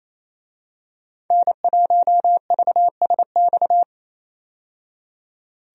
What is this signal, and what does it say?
Morse code, heard as a keyed tone, about 28 wpm, 700 Hz. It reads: N1VSX
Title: USA Call Signs - 5 Characters - 28wpm